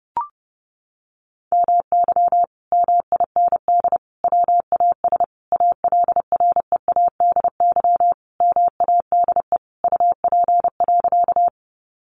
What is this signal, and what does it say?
Morse code, heard as a keyed tone, about 28 wpm, 700 Hz. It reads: MY MIND WAS ALREADY MADE UP.
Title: Sentences from Top 700 Words 30wpm